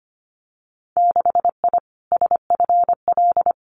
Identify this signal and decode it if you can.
Morse code, heard as a keyed tone, about 25 wpm, 700 Hz. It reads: 6I SFL
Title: Winter Field Day Exchanges Encore 25wpm